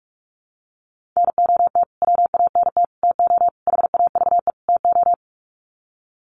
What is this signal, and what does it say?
Morse code, heard as a keyed tone, about 44 wpm, 700 Hz. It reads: NOT WANT TO HAVE TO
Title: Sets of 5 Words Encore2 45wpm